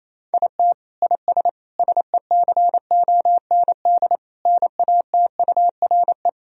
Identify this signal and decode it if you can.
Morse code, heard as a keyed tone, about 28 wpm, 700 Hz. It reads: IT IS SECOND NATURE
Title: Sentences from Top 1000 Words - Review Part 1 - 28wpm